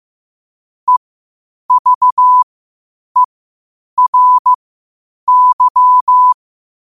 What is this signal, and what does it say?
Morse code, heard as a keyed tone, about 15 wpm, spaced as for 8 wpm, 1000 Hz. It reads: EVERY
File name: qotdfiles / 8wpm